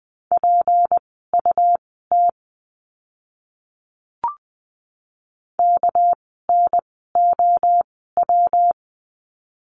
Morse code, heard as a keyed tone, about 20 wpm, spaced as for 14 wpm, 700 Hz. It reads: PUT KNOW
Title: 237 Lesson 38f - Letter q - Words 20wpm